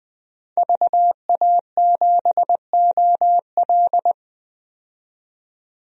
Morse code, heard as a keyed tone, about 20 wpm, 700 Hz. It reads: VA7OL
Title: DXCC-100 Call Signs - 5 Characters - 20wpm